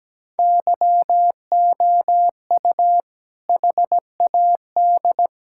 Morse code, heard as a keyed tone, about 17 wpm, 700 Hz. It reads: YOU HAD
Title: Sentences from Top 1600 Words 17wpm